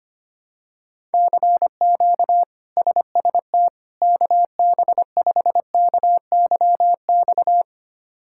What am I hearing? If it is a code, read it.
Morse code, heard as a keyed tone, about 25 wpm, 700 Hz. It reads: CQ SST KB5KYX
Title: K1USN SST QSO Practice - 25wpm